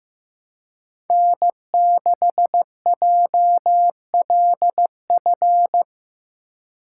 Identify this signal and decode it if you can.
Morse code, heard as a keyed tone, about 15 wpm, 700 Hz. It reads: N6JLF
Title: USA Call Signs - 3x Character Spacing - 15wpm